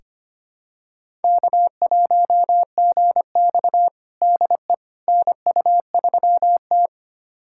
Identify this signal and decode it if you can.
Morse code, heard as a keyed tone, about 25 wpm, 700 Hz. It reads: K1GX DE NU3T
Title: USA Callsign DE Callsign - 4 Characters - 25wpm